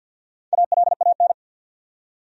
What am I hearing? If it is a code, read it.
Morse code, heard as a keyed tone, about 51 wpm, 700 Hz. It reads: ALAN
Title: USA Common Names - Max 5 Letters - 50wpm